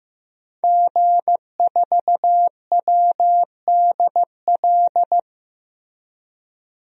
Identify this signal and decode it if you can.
Morse code, heard as a keyed tone, about 15 wpm, 700 Hz. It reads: G4WDL
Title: DXCC-100 Call Signs - 5 Characters - 15wpm